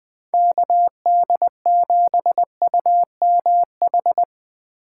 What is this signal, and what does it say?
Morse code, heard as a keyed tone, about 20 wpm, 700 Hz. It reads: KD7UMH
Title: USA Call Signs - 6 Characters - 3x Character Spacing - 20wpm